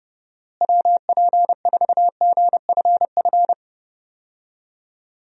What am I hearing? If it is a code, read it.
Morse code, heard as a keyed tone, about 30 wpm, 700 Hz. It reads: WP4GFF
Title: USA Call Signs - 6 Characters - Repeated 3x - 30wpm